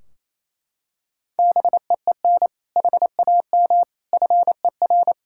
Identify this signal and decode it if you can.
Morse code, heard as a keyed tone, about 28 wpm, 700 Hz. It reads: BEEN HAM FER
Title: K1USN SST QSO Practice 28wpm